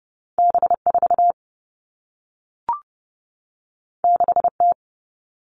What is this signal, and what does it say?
Morse code, heard as a keyed tone, about 30 wpm, 700 Hz. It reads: B 4 6 T
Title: All Cut Numbers - 2-Digit 30wpm